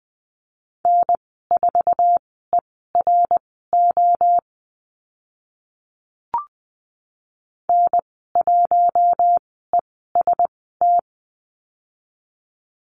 Morse code, heard as a keyed tone, about 20 wpm, spaced as for 14 wpm, 700 Hz. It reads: N4ERO N1EST
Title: 076 Lesson 13g - Letter L - Call Signs 20wpm